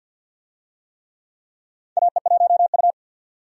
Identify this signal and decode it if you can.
Morse code, heard as a keyed tone, about 52 wpm, 700 Hz. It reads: AE1U
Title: USA Call Signs - 4 Characters - 50wpm